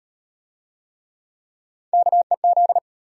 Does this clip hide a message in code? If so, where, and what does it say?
Yes: Morse code from 1.9 to 2.8 s, heard as a keyed tone, about 33 wpm, 700 Hz. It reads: KEZ